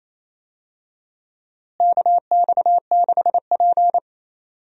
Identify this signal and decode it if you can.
Morse code, heard as a keyed tone, about 28 wpm, 700 Hz. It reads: KX6P
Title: USA Call Signs - 4 Characters - Repeated 3x - 28wpm